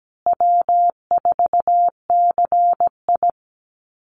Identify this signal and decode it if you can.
Morse code, heard as a keyed tone, about 17 wpm, 700 Hz. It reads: W4CI
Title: Easy Call Signs Repeated 3x - 17wpm